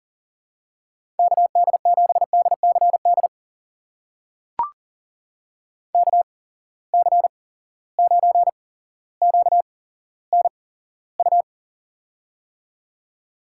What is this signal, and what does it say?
Morse code, heard as a keyed tone, about 41 wpm, spaced as for 17 wpm, 700 Hz. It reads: KD7DCD KC9QNU
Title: USA Call Signs - 6 Characters - 8x Character Spacing - 40wpm